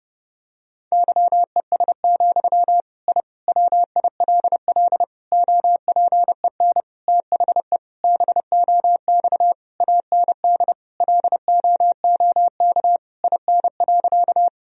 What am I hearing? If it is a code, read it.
Morse code, heard as a keyed tone, about 30 wpm, 700 Hz. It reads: YES, I WILL OPEN THE BOX AND LOOK IN.
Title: Sentences from Top 1000 Words - Review Part 2 - 30wpm